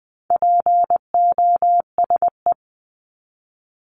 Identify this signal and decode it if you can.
Morse code, heard as a keyed tone, about 20 wpm, 700 Hz. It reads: POSE
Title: Top 500 Words 20wpm